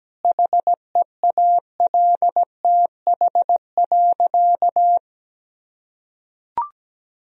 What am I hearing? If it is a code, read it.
Morse code, heard as a keyed tone, about 17 wpm, 700 Hz. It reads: HEALTH.
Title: Sentences from Top 1000 Words - Review Part 3 - 17wpm